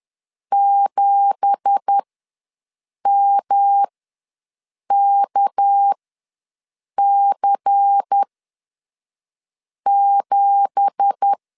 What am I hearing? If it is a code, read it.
Morse code, heard as a keyed tone, about 10 wpm, spaced as for 5 wpm, 800 Hz. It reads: 7MKC7
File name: LEZIONI / 9 2867 ws 10wpm 4215